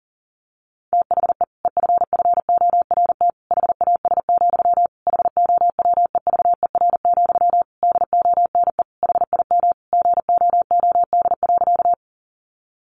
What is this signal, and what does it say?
Morse code, heard as a keyed tone, about 40 wpm, 700 Hz. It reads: THE EFFORT HAS, HOWEVER, DONE HIM GOOD.
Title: Sentences from Top 1200 Words 40wpm